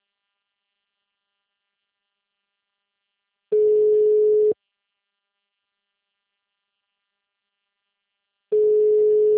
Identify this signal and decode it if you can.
Ringback tone (Europe, 425 Hz).
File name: out- -2002- - - .35292